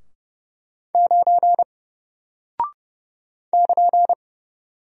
Morse code, heard as a keyed tone, about 30 wpm, 700 Hz. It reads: 9 <KN>
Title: Single Character (Letter:Number:Punctuation:Prosign) Alt 30wpm